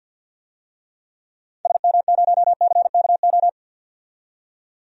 Morse code, heard as a keyed tone, about 52 wpm, 700 Hz. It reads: SM0CKO
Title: DXCC-100 Call Signs - 6 Characters - 50wpm